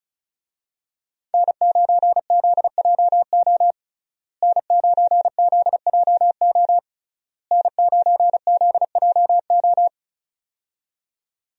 Morse code, heard as a keyed tone, about 35 wpm, 700 Hz. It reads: N9ZJO N9ZJO N9ZJO
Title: USA Call Signs - 5 Characters - Repeated 3x - 35wpm